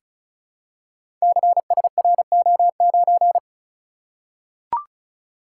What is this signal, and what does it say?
Morse code, heard as a keyed tone, about 36 wpm, 700 Hz. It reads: CSRO9
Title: Code Groups - 5 Letter:Numbers - 35wpm